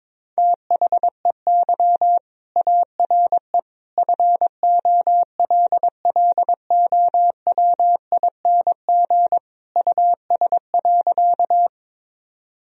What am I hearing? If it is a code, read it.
Morse code, heard as a keyed tone, about 22 wpm, 700 Hz. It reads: THEY ARE FOLLOWING US.
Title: Sentences from Top 1000 Words - Review Part 4 - 22wpm